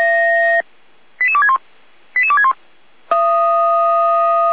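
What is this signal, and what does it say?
Five-tone selective call (ZVEI), heard as quick runs of five tones, tones of about 70 ms. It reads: 88261 88261
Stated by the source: ZVEI five-tone selective call 88335, received on an SDR, from the Feuerwehr Orlamünde (fire brigade) audio server.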